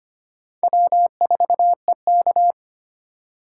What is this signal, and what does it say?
Morse code, heard as a keyed tone, about 33 wpm, 700 Hz. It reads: W4EK